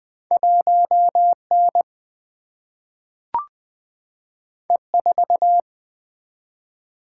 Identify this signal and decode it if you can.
Morse code, heard as a keyed tone, about 20 wpm, 700 Hz. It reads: 1N E4
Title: Single-Letter Single-Number 20wpm